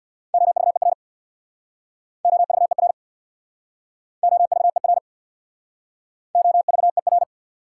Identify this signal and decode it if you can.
Morse code, heard as a keyed tone, about 69 wpm, 690 Hz. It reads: OVER OVER OVER OVER